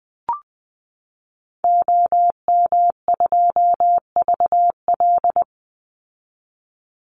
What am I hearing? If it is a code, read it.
Morse code, heard as a keyed tone, about 20 wpm, 700 Hz. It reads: OM2VL
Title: Call Signs Repeated 2x - 20wpm